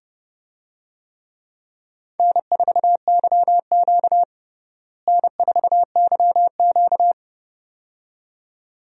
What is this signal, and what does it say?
Morse code, heard as a keyed tone, about 30 wpm, 700 Hz. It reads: N4YQ N4YQ
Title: USA Call Signs - 4 Characters - Repeated 2x - 30wpm